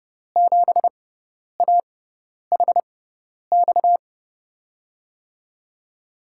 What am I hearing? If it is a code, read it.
Morse code, heard as a keyed tone, about 30 wpm, spaced as for 10 wpm, 700 Hz. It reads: 7AHX
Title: USA Call Signs - 6 Characters - 6x Character Spacing - 30wpm